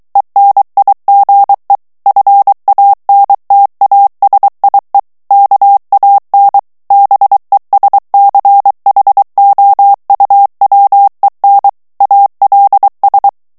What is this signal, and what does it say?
Morse code, heard as a keyed tone, about 23 wpm, 800 Hz. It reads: ENIGE FANTASIE KAN BESCHOUWEN ALS
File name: HOLLANDAIS texte clair 1 vitesse 22 mots minute